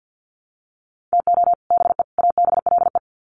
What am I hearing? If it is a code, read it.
Morse code, heard as a keyed tone, about 50 wpm, 700 Hz. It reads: TO BE ABLE
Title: Sets of 3 Words 50wpm